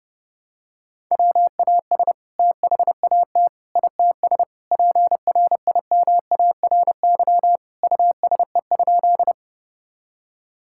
Morse code, heard as a keyed tone, about 30 wpm, 700 Hz. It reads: WAS THAT ITS PRIMARY USE?
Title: Sentences from Top 1000 Words - Review Part 5 - 30wpm